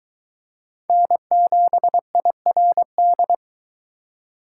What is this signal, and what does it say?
Morse code, heard as a keyed tone, about 22 wpm, 700 Hz. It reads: N7IRD